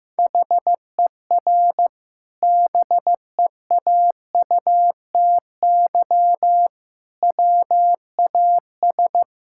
Morse code, heard as a keyed tone, about 15 wpm, 700 Hz. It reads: HER BEAUTY WAS
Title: Sentences from Top 1000 Words - Review Part 6 - 15wpm